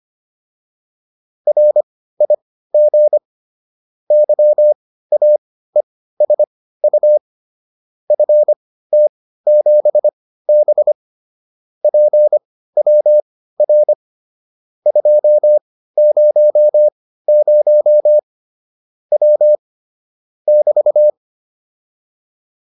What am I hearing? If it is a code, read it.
Morse code, heard as a keyed tone, about 25 wpm, spaced as for 15 wpm, 600 Hz. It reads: RIG YAESU FT7B PWR 200 W <BT>